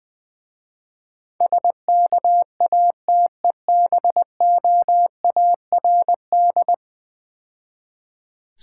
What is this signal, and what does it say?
Morse code, heard as a keyed tone, about 22 wpm, 700 Hz. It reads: SKATEBOARD
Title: Easy Words - Speed-Racing with Hints - 20wpm